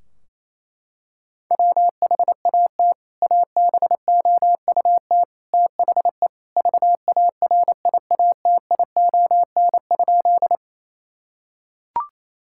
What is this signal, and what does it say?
Morse code, heard as a keyed tone, about 28 wpm, 700 Hz. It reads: WHAT ABOUT THE VARIATION?
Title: Sentences from Top 1600 Words 28wpm